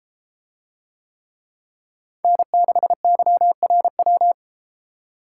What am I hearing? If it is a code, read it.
Morse code, heard as a keyed tone, about 31 wpm, 700 Hz. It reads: N6YRW